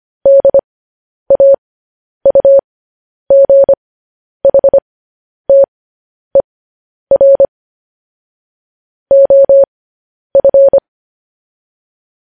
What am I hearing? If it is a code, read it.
Morse code, heard as a keyed tone, about 25 wpm, spaced as for 10 wpm, 550 Hz. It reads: DAUGHTER OF
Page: news headlines in morse code at 10 wpm